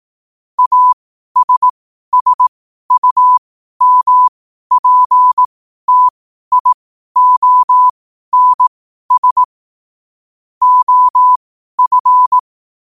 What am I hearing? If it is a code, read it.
Morse code, heard as a keyed tone, about 18 wpm, spaced as for 12 wpm, 1000 Hz. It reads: ASSUMPTIONS OF
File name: qotdfiles / 12wpm